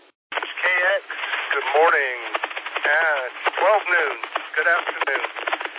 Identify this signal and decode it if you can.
Emergency Alert System (SAME) tones, heard as harsh buzzing bursts, with speech. It is heard from 1.1 to 4.8 s.